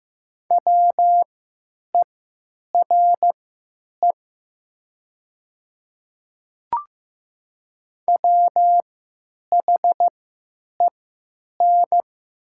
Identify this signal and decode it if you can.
Morse code, heard as a keyed tone, about 15 wpm, spaced as for 9 wpm, 700 Hz. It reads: WERE WHEN
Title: 3x Character Spacing - Top 100 Words - 15wpm